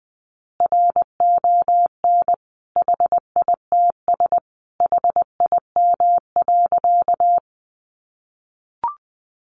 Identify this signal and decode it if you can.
Morse code, heard as a keyed tone, about 20 wpm, 700 Hz. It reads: RON HITS HIM.
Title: Additional Easy Phrases 20wpm